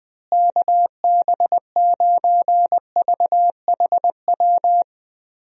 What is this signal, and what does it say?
Morse code, heard as a keyed tone, about 21 wpm, 700 Hz. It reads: KB9VHW
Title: USA Call Signs - 6 Characters - Repeated 2x - 20wpm